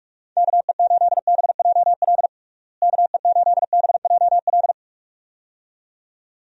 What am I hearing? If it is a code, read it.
Morse code, heard as a keyed tone, about 46 wpm, 700 Hz. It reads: KE8BJL KE8BJL
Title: USA Call Signs - 6 Characters - Repeated 2x - 45wpm